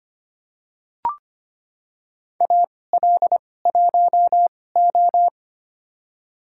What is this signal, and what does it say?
Morse code, heard as a keyed tone, about 25 wpm, spaced as for 18 wpm, 700 Hz. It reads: AL1O
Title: USA Call Signs - 2x Character Spacing - 25wpm